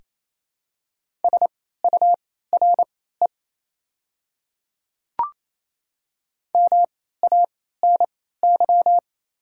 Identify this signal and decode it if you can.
Morse code, heard as a keyed tone, about 28 wpm, spaced as for 16 wpm, 700 Hz. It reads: SURE MANY
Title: 3x Character Spacing - Top 500 Words - 28wpm